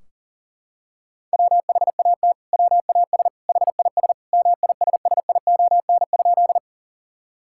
Morse code, heard as a keyed tone, about 40 wpm, 700 Hz. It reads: WHAT WAS HIS MISSION?
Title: Sentences from Top 1600 Words 40wpm